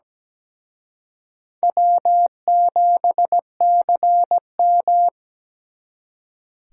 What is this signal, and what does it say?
Morse code, heard as a keyed tone, about 20 wpm, 700 Hz. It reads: W7CM
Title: Speed Racing - USA Call Signs 17wpm